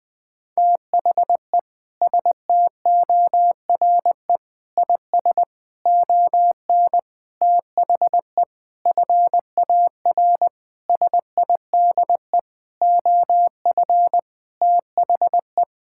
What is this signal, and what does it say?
Morse code, heard as a keyed tone, about 19 wpm, 700 Hz. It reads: THE STORE IS ON THE FAR SIDE OF THE
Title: Sentences with 6-Word Phrases - 20wpm